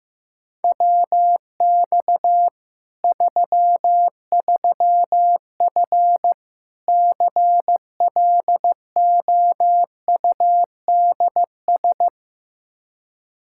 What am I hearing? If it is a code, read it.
Morse code, heard as a keyed tone, about 16 wpm, 700 Hz. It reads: WX 33F CLOUDS
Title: Practice QSO - Exchange 2 - Info - 15wpm